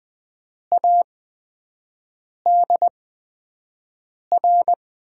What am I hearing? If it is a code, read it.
Morse code, heard as a keyed tone, about 20 wpm, spaced as for 5 wpm, 700 Hz. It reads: ADR
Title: USA Call Signs - 5 Characters - 8x Character Spacing - 20wpm